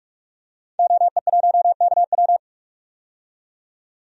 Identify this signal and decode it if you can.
Morse code, heard as a keyed tone, about 48 wpm, 700 Hz. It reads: OE1KW
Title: DXCC-100 Call Signs - 5 Characters - 45wpm